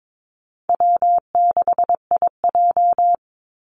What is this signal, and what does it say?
Morse code, heard as a keyed tone, about 22 wpm, 700 Hz. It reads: W6IJ
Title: Call Signs Repeated 2x - 22wpm